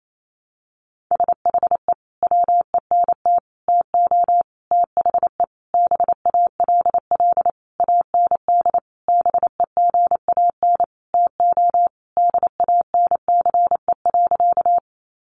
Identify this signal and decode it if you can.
Morse code, heard as a keyed tone, about 28 wpm, 700 Hz. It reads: SHE WENT TO THE BALL AND BEGAN TO DANCE.
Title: Sentences from Top 900 Words 28wpm